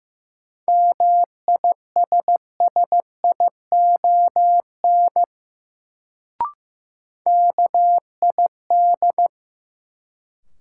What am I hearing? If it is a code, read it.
Morse code, heard as a keyed tone, about 15 wpm, 700 Hz. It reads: MISSION KID
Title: Sentences from Top 1600 Words - New Word Drill Alt - 15wpm